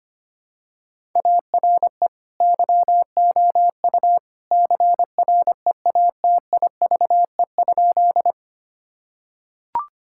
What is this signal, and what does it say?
Morse code, heard as a keyed tone, about 25 wpm, 700 Hz. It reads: ARE YOU CREATIVE?
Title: Sentences from Top 1000 Words - Review Part 5 - 25wpm